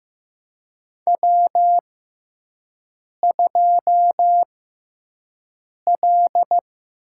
Morse code, heard as a keyed tone, about 15 wpm, spaced as for 5 wpm, 700 Hz. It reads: W2L
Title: USA Call Signs - 5 Characters - 6x Character Spacing - 15wpm